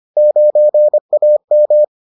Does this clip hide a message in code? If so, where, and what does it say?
Yes: Morse code from 0.2 to 1.8 s, heard as a keyed tone, about 25 wpm, 600 Hz. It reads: 9AM